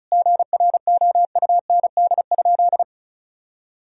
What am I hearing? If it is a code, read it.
Morse code, heard as a keyed tone, about 36 wpm, 700 Hz. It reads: GROUND?
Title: Sentences from Top 1000 Words - Review Part 2 - 35wpm